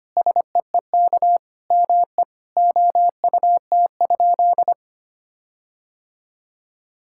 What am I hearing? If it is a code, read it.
Morse code, heard as a keyed tone, about 25 wpm, 700 Hz. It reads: SEEK ME OUT?
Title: Sentences from Top 1600 Words 25wpm